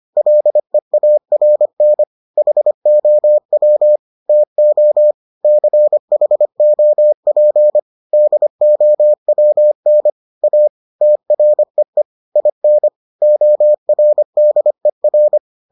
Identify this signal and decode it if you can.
Morse code, heard as a keyed tone, about 25 wpm, 600 Hz. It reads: LEARN HOW TO CHOP DOWN A TREE IN ORDER